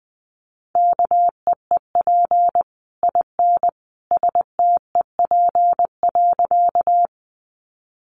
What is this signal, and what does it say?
Morse code, heard as a keyed tone, about 20 wpm, 700 Hz. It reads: KEEP IN STEP.
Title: Sentences from Top 400 Words 20wpm